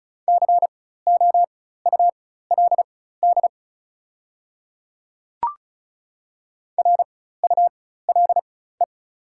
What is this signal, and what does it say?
Morse code, heard as a keyed tone, about 34 wpm, spaced as for 16 wpm, 700 Hz. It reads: COULD RULE
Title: 4x Character Spacing - Top 500 Words - 35wpm